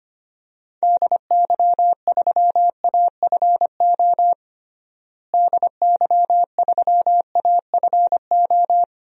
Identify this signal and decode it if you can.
Morse code, heard as a keyed tone, about 25 wpm, 700 Hz. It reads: DY3AFO DY3AFO
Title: DXCC-100 Call Signs - 6 Characters - Repeated 2x - 25wpm